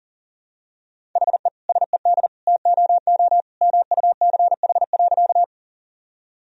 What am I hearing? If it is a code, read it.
Morse code, heard as a keyed tone, about 40 wpm, 700 Hz. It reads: HE SED TOO MUCH.
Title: Sentences from Top 200 Words and Abbreviations 40wpm